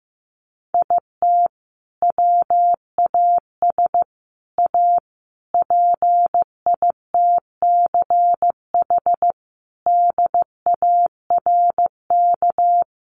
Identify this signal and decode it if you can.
Morse code, heard as a keyed tone, about 15 wpm, 700 Hz. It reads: IT WAS A PITCH DARK
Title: Sentences from Top 1300 Words 15wpm